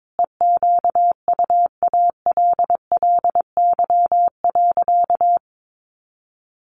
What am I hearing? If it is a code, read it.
Morse code, heard as a keyed tone, about 22 wpm, 700 Hz. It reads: EQUALLY.
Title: Sentences from Top 1400 Words 22wpm